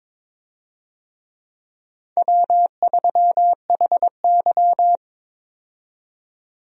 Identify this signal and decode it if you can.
Morse code, heard as a keyed tone, about 22 wpm, 700 Hz. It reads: W3HY
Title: USA Call Signs - 4 Characters - 22wpm